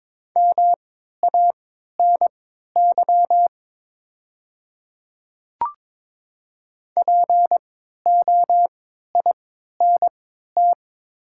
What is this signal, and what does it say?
Morse code, heard as a keyed tone, about 22 wpm, spaced as for 12 wpm, 700 Hz. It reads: MANY POINT
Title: 3x Character Spacing - Top 300 Words - 22wpm